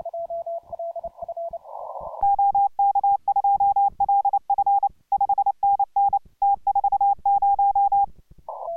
Morse code, heard as a keyed tone, about 29 wpm, 800 Hz. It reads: OK2LF 5NN T40